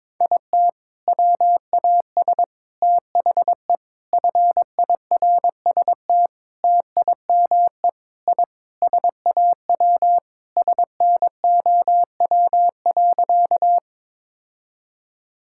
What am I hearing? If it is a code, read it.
Morse code, heard as a keyed tone, about 22 wpm, 700 Hz. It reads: IT WAS THE FIRST TIME I SAW SNOW.
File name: practice / 22wpm / Sentences with 4-Word Phrases - 22wpm